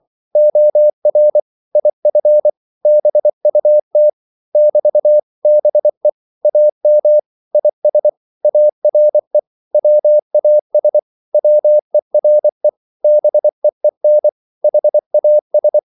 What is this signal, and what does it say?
Morse code, heard as a keyed tone, about 24 wpm, 600 Hz. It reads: OR IF BUT <BT> BE AM IS ARE WAS WERE BEEN HAS
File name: CW / Practice / 100MostCommonEnglishWords24